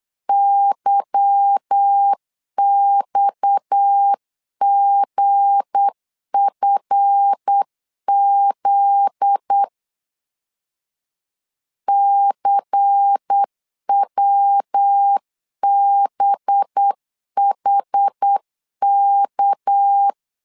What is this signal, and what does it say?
Morse code, heard as a keyed tone, about 8 wpm, 800 Hz. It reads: YXGFZ CWBHK
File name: CW / DET / 1 9 1500C S A 8wpm 5637